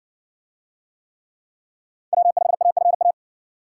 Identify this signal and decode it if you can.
Morse code, heard as a keyed tone, about 55 wpm, 700 Hz. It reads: W5AFA